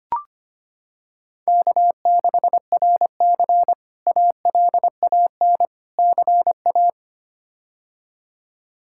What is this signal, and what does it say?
Morse code, heard as a keyed tone, about 25 wpm, 700 Hz. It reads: K6RC ALAN CA
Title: K1USN SST QSO Practice - 25wpm